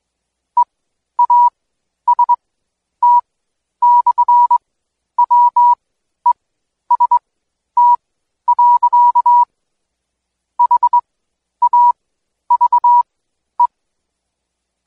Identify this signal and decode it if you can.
Morse code, heard as a keyed tone, about 21 wpm, spaced as for 11 wpm, 970 Hz. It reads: EAST/WEST. HAVE
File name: Code Quick / AudioDownld2 / Seg120-Day 25-DayDrill 25